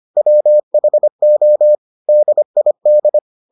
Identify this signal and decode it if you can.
Morse code, heard as a keyed tone, about 25 wpm, 600 Hz. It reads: WHO DID